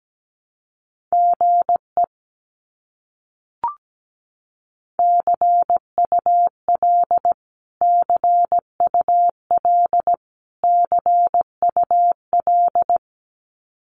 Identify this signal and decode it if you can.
Morse code, heard as a keyed tone, about 17 wpm, 700 Hz. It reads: GE CUL CUL CUL
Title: Abbreviations Repeated 3x - 17wpm